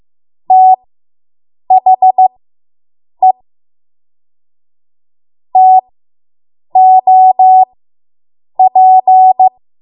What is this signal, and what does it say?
Morse code, heard as a keyed tone, about 15 wpm, spaced as for 7 wpm, 750 Hz. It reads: THE TOP